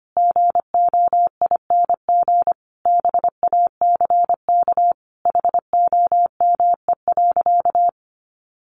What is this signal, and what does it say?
Morse code, heard as a keyed tone, about 25 wpm, 700 Hz. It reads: GOING BACK HOME.
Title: Sentences from Top 500 Words 25wpm